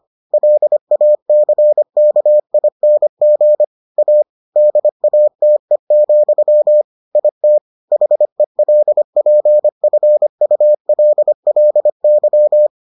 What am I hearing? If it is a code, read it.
Morse code, heard as a keyed tone, about 25 wpm, 600 Hz. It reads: LACKING A DATE, IT HELPFULLY